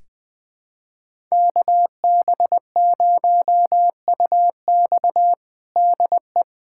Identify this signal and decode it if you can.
Morse code, heard as a keyed tone, about 20 wpm, 700 Hz. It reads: KB0UX DE
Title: USA Callsign DE Callsign 20wpm